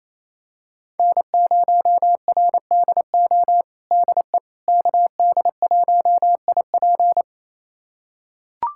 Morse code, heard as a keyed tone, about 28 wpm, 700 Hz. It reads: N0RDO DE KD1IP
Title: USA Callsign DE Callsign - 5 Characters - 28wpm